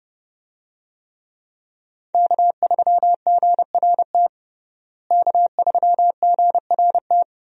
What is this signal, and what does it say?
Morse code, heard as a keyed tone, about 32 wpm, 700 Hz. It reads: K3GRT K3GRT
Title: USA Call Signs - 5 Characters - Repeated 3x - 30wpm